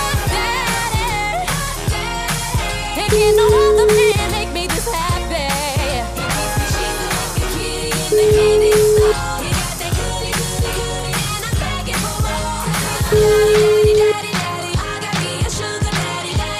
Ringback tone (Europe, 425 Hz), with noise.